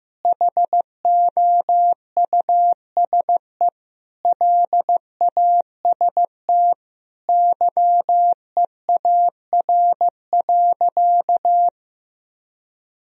Morse code, heard as a keyed tone, about 14 wpm, 700 Hz. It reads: HOUSE LAST YEAR.
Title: Sentences from Top 1000 Words - Review Part 1 - 15wpm